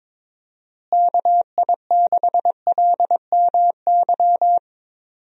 Morse code, heard as a keyed tone, about 21 wpm, 700 Hz. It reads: KI6LMY
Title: USA Call Signs - 6 Characters - Repeated 3x - 22wpm